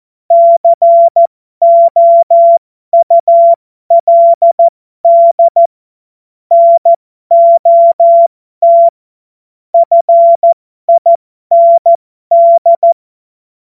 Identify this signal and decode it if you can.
Morse code, heard as a keyed tone, about 14 wpm, spaced as for 12 wpm, 680 Hz. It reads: COULD NOT FIND
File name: Morsekurs / Witze